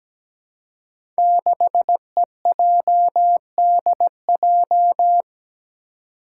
Morse code, heard as a keyed tone, about 17 wpm, 700 Hz. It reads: 6EJDJ
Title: Five Challenging Letter-Number Combinations 17wpm